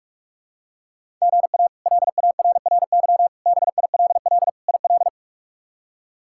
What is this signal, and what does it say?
Morse code, heard as a keyed tone, about 46 wpm, 700 Hz. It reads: GA LARRY BILL IL
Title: K1USN SST QSO Practice - 45wpm